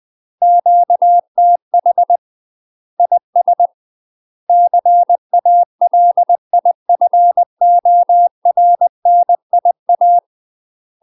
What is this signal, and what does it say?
Morse code, heard as a keyed tone, about 20 wpm, 700 Hz. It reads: QTH IS CALIFORNIA
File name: UK MASTER QSOs TRAINING/20WPM